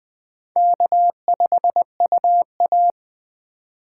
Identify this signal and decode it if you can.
Morse code, heard as a keyed tone, about 20 wpm, 700 Hz. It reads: K5UA